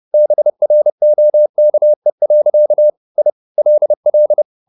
Morse code, heard as a keyed tone, about 30 wpm, 600 Hz. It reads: BROKE. I LL